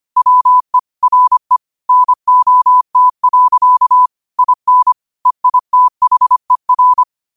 Morse code, heard as a keyed tone, about 25 wpm, 1000 Hz. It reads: WE RE NOT. IN EITHER
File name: qotdfiles / 25wpm